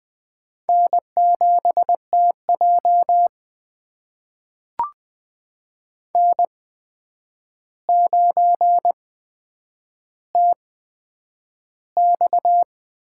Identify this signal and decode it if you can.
Morse code, heard as a keyed tone, about 20 wpm, spaced as for 9 wpm, 700 Hz. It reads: N7TJ N9TX
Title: USA Call Signs - 4 Characters - 8x Character Spacing - 20wpm